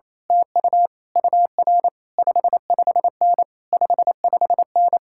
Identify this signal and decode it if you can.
Morse code, heard as a keyed tone, about 26 wpm, 700 Hz. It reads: TU UR 55N 55N
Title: Speed-Racing Common QSO Phrases - 28wpm